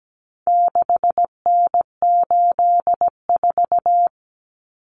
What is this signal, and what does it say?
Morse code, heard as a keyed tone, about 17 wpm, 700 Hz. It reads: 6N84
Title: Common Cut Numbers - 4-Digit 17wpm